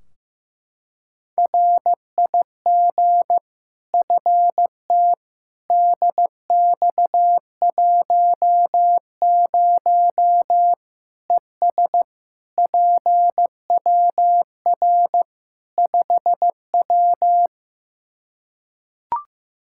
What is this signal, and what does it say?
Morse code, heard as a keyed tone, about 15 wpm, 700 Hz. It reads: RIG FT DX10 ES PWR 5W
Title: Practice QSO - Exchange 2 - Info - 15wpm